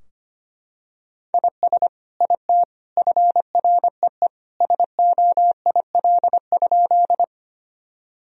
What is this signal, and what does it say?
Morse code, heard as a keyed tone, about 25 wpm, 700 Hz. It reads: IS IT FREE SOIL?
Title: Sentences from Top 1000 Words - Review Part 4 - 25wpm